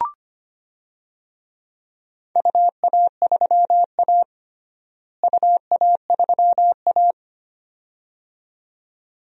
Morse code, heard as a keyed tone, about 25 wpm, 700 Hz. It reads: UA3A UA3A
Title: DXCC-100 Call Signs - 4 Characters - Repeated 2x - 25wpm